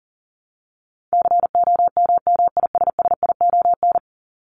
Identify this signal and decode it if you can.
Morse code, heard as a keyed tone, about 40 wpm, 700 Hz. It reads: COMMISSION
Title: Top 1251-1500 Words 40wpm